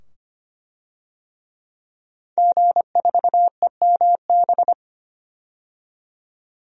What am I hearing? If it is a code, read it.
Morse code, heard as a keyed tone, about 25 wpm, 700 Hz. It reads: G4EMB
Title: DXCC-100 Call Signs - 5 Characters - 25wpm